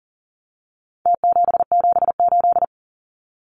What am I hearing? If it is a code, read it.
Morse code, heard as a keyed tone, about 40 wpm, 700 Hz. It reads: T778
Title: Numbers - Contest Style - 4-Digit 40wpm